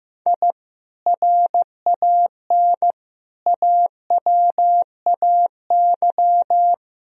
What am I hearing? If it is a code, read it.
Morse code, heard as a keyed tone, about 15 wpm, 700 Hz. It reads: I RAN AWAY
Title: Sentences from Top 1000 Words - Review Part 3 - 15wpm